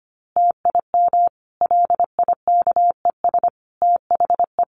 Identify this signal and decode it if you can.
Morse code, heard as a keyed tone, about 25 wpm, 700 Hz. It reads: TIM LIKES THE
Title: Easy Phrases 25wpm